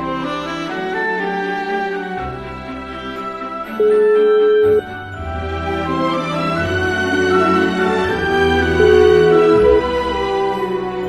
Ringback tone (Europe, 425 Hz), with noise.